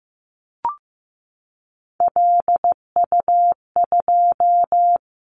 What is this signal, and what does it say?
Morse code, heard as a keyed tone, about 15 wpm, 700 Hz. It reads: LU2
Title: Callsign Prefixes for Top 100 DXCC - 15wpm